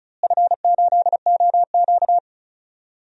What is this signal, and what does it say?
Morse code, heard as a keyed tone, about 36 wpm, 700 Hz. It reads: F8OQ
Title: DXCC-100 Call Signs - 4 Characters - Repeated 3x - 35wpm